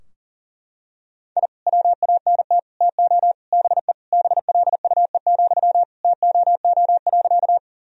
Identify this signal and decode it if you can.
Morse code, heard as a keyed tone, about 40 wpm, 700 Hz. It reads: I WANT TO BE BLUE, TOO.
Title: Sentences from Top 1000 Words - Review Part 3 - 40wpm